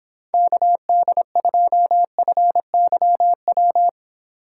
Morse code, heard as a keyed tone, about 30 wpm, 700 Hz. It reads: KD2FYW